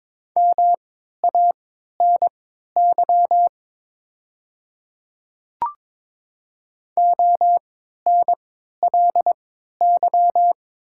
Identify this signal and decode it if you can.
Morse code, heard as a keyed tone, about 22 wpm, spaced as for 12 wpm, 700 Hz. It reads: MANY ONLY
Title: 3x Character Spacing - Top 200 Words - 22wpm